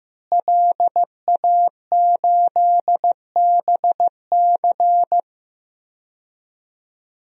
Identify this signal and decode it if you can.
Morse code, heard as a keyed tone, about 15 wpm, 700 Hz. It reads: LA8BC
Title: DXCC-100 Call Signs - 5 Characters - Repeated 2x - 15wpm